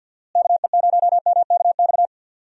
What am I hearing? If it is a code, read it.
Morse code, heard as a keyed tone, about 51 wpm, 700 Hz. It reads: KE0MKX
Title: USA Call Signs - 6 Characters - Repeated 3x - 50wpm